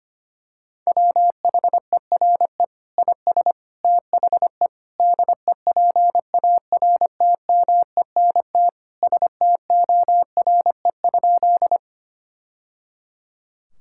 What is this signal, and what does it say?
Morse code, heard as a keyed tone, about 25 wpm, 700 Hz. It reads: WHERE IS THE DEPARTMENT STORE?
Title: Sentences from Top 1000 Words - Review Part 5 - 25wpm